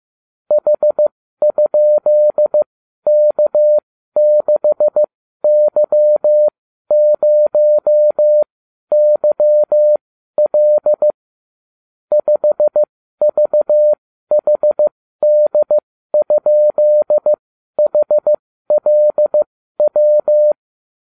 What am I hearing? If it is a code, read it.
Morse code, heard as a keyed tone, about 15 wpm, spaced as for 12 wpm, 600 Hz. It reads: H?K6Y0YL 5VHD?HLW